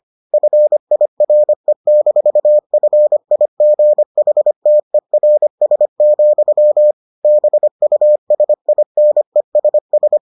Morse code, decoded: FIRE-FIGHTERS, BUSINESS

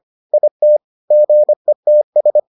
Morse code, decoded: IT GETS

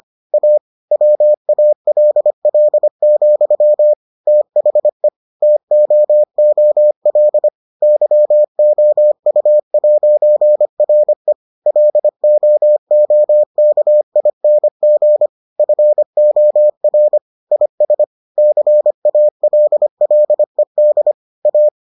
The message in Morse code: A WALL, THE TOOL YOU'RE LOOKING FOR IS CALLED A

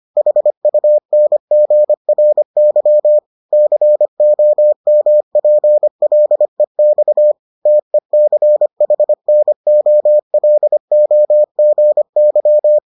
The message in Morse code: HUNGRY COMPLEX TECHNOLOGY

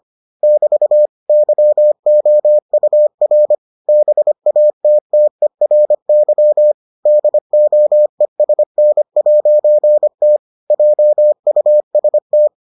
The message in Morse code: <BT> YOUR BATTERY DOESN'T JUST

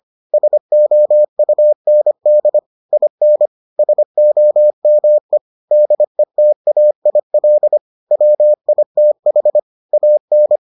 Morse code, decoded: SOUND IN SOME DETAIL WITH AN

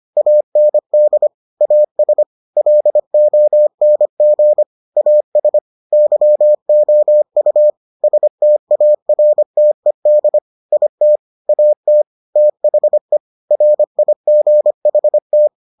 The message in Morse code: AND AS LONG AS YOU STARTED IT AT THE RIGHT

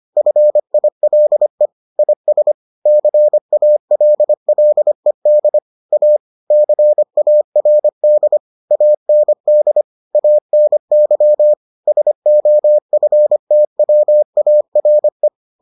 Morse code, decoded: FILE IS CALLED A CARD AND ANY SOFTWARE

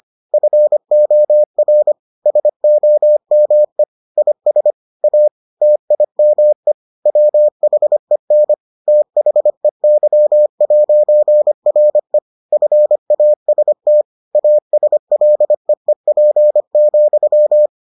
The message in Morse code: FOR SOME IS A TIME WHEN THEY'RE FAST ASLEEP,

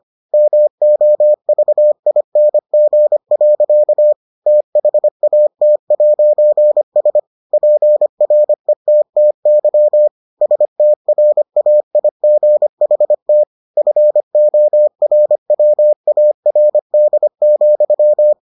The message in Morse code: MOVING. THAT'S PRETTY STRAIGHT FORWARD,

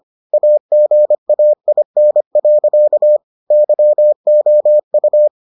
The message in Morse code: AGAIN. YOU

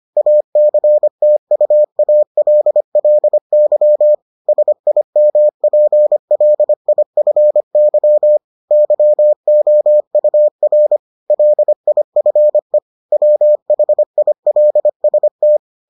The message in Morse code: ACTUALLY SIMPLIFY YOUR LIFE WHILST

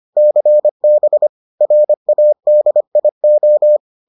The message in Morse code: CB RADIO

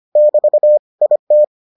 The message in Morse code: <BT> IT

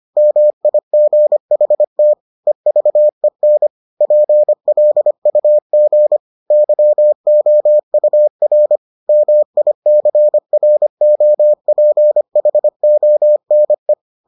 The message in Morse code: MIGHT EVEN PLUG YOUR MICROPHONE

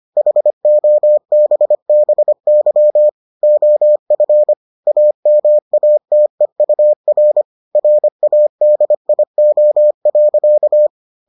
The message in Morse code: HOBBY OF AMATEUR RADIO.

